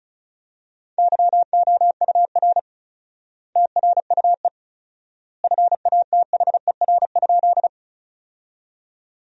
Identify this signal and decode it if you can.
Morse code, heard as a keyed tone, about 36 wpm, 700 Hz. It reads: YOUR TRUE FATHER?
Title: 4x Word-Spacing - Sentences from Top 400 Words - 35wpm